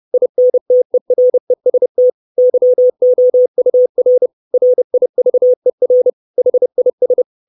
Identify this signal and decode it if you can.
Morse code, heard as a keyed tone, about 30 wpm, 500 Hz. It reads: INTEREST YOUR RIVER HIS
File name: 632 Words 30WPM